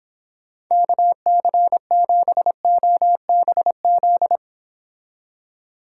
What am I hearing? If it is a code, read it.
Morse code, heard as a keyed tone, about 26 wpm, 700 Hz. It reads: KC7OBZ